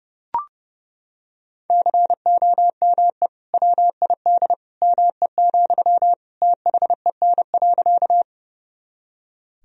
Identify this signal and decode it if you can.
Morse code, heard as a keyed tone, about 30 wpm, 700 Hz. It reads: COME WID ME, THEN.
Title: Sentences from Top 200 Words and Abbreviations 30wpm